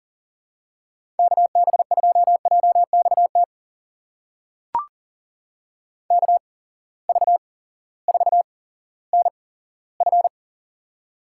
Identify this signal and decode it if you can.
Morse code, heard as a keyed tone, about 42 wpm, spaced as for 19 wpm, 700 Hz. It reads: KB2JXT KV4NF
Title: USA Call Signs - 8x Character Spacing - 40wpm